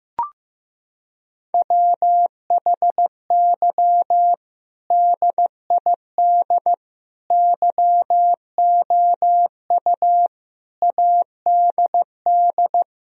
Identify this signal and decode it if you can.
Morse code, heard as a keyed tone, about 15 wpm, 700 Hz. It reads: WHY DID YOU ADD
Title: Sentences from Top 1000 Words - Review Part 1 - 15wpm